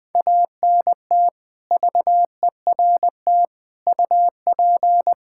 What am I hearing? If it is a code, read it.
Morse code, heard as a keyed tone, about 20 wpm, 700 Hz. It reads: ANT VERT UP
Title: Practice QSO - Exchange 2 - Info - 20wpm